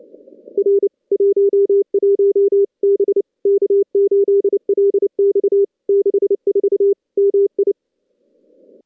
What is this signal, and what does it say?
Morse code, heard as a keyed tone, about 29 wpm, 400 Hz. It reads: R 11 B K8LX 64 MI
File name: contests / CW-SS-k5tr- -2100